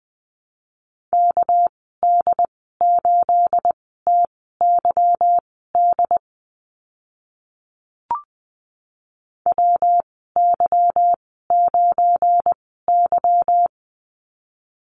Morse code, heard as a keyed tone, about 20 wpm, spaced as for 14 wpm, 700 Hz. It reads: KD8TYD WY9Y
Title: 213 Lesson 34g - Number 8 - Call Signs 20wpm